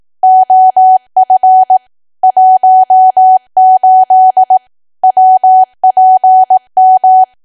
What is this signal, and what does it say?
Morse code, heard as a keyed tone, about 18 wpm, 750 Hz. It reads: OF 18 WPM
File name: Archive / 18 WPM / 18WPM